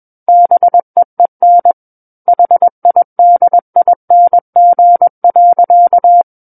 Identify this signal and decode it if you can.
Morse code, heard as a keyed tone, about 21 wpm, 700 Hz. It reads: BEEN HIDING.
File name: Book One / 12-Chapter 12 - 21 WPM